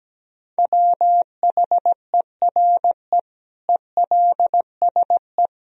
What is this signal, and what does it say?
Morse code, heard as a keyed tone, about 17 wpm, 700 Hz. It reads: WHERE ELSE
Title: Sentences from Top 1000 Words - Review Part 1 - 17wpm